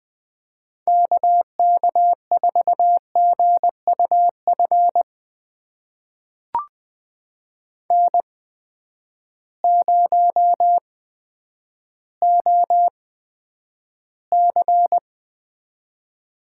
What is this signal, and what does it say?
Morse code, heard as a keyed tone, about 20 wpm, spaced as for 10 wpm, 700 Hz. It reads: KK4GUF N0OC
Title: USA Call Signs - 8x Character Spacing - 20wpm